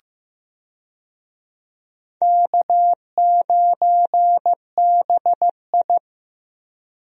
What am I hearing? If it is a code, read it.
Morse code, heard as a keyed tone, about 15 wpm, 700 Hz. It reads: K9BI
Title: USA Call Signs - 4 Characters - Repeated 2x - 15wpm